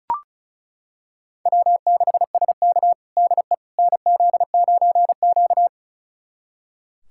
Morse code, heard as a keyed tone, about 36 wpm, 700 Hz. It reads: W6SK DE NZ9Q
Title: USA Callsign DE Callsign - 4 Characters - 35wpm